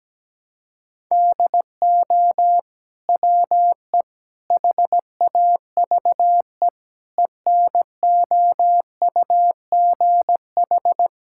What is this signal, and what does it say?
Morse code, heard as a keyed tone, about 17 wpm, 700 Hz. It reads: DO WE HAVE ENOUGH
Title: Sentences from Top 1000 Words - Review Part 5 - 17wpm